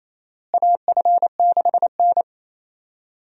Morse code, heard as a keyed tone, about 28 wpm, 700 Hz. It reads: AF6N
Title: USA Call Signs - 4 Characters - Repeated 2x - 28wpm